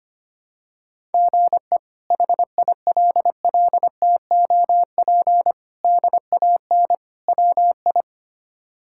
Morse code, heard as a keyed tone, about 25 wpm, 700 Hz. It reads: GE HILLTOP DAN WI
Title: K1USN SST QSO Practice - 25wpm